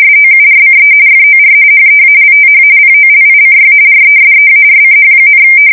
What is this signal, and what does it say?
RTTY, about 45.45 baud. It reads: SOURCE FOR HEALTH POLICY RESEARCH,
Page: 170 Shift 2125/2295